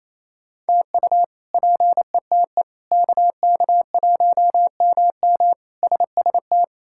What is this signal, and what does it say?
Morse code, heard as a keyed tone, about 28 wpm, 700 Hz. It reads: TU PETE KK1MM SST
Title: K1USN SST QSO Practice - 28wpm